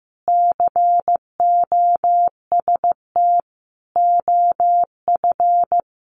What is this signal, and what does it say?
Morse code, heard as a keyed tone, about 15 wpm, 700 Hz. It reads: COST OF
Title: Sentences from Top 1400 Words 15wpm